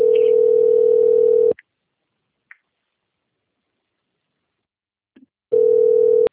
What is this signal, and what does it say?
Ringback tone (North America), 440 + 480 Hz.